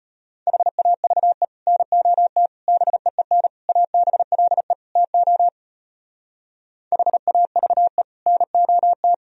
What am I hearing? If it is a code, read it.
Morse code, heard as a keyed tone, about 33 wpm, 700 Hz. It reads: HAVE NOT BEEN ABLE TO HAVE NOT